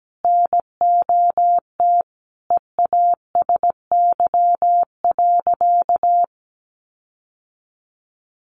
Morse code, heard as a keyed tone, about 17 wpm, 700 Hz. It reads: NOT EASY.
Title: Sentences from Top 1500 Words 17wpm